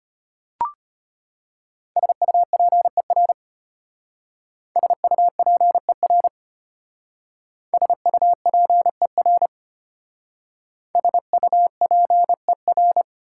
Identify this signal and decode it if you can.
Morse code, heard as a keyed tone, about 32 wpm, 690 Hz. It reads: SUPER SUPER SUPER SUPER